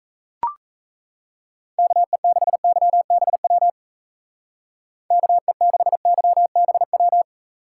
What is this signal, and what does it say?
Morse code, heard as a keyed tone, about 33 wpm, 700 Hz. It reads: KE6YBW KE6YBW
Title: Speed Racing - USA 2x3 Call Signs 28wpm